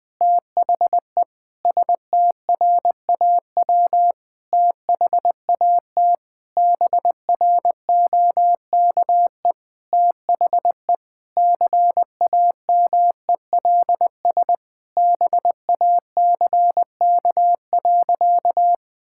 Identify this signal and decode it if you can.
Morse code, heard as a keyed tone, about 20 wpm, 700 Hz. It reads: THE STRAW THAT BROKE THE CAMELS BACK.
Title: Sentences with 8-Word Phrases - Spoken Beginning - 20wpm